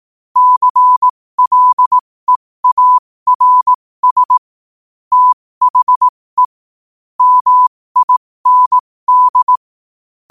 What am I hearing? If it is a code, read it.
Morse code, heard as a keyed tone, about 18 wpm, spaced as for 15 wpm, 1000 Hz. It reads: CLEARS THE MIND
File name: qotdfiles / 15wpm